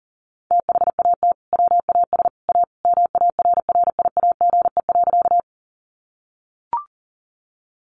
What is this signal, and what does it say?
Morse code, heard as a keyed tone, about 39 wpm, 700 Hz. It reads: THAT WAS A MARRIAGE.
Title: Sentences from Top 800 Words 40wpm